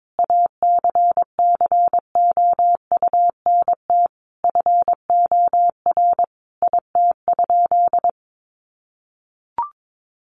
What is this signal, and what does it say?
Morse code, heard as a keyed tone, about 22 wpm, 700 Hz. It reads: ACCOUNT FOR IT?
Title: Sentences from Top 900 Words 22wpm